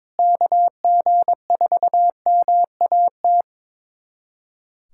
Morse code, heard as a keyed tone, about 22 wpm, 700 Hz. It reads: KG4MAT
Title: USA Callsign DE Callsign 22wpm